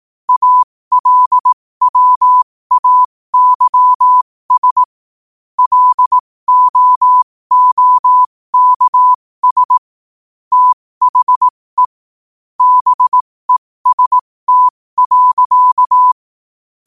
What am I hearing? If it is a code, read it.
Morse code, heard as a keyed tone, about 18 wpm, spaced as for 15 wpm, 1000 Hz. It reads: ALWAYS LOOKS THE BEST.
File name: 15wpm